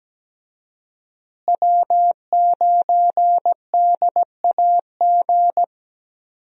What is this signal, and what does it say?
Morse code, heard as a keyed tone, about 17 wpm, 700 Hz. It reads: W9DAG
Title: USA Call Signs - 5 Characters - Repeated 3x - 17wpm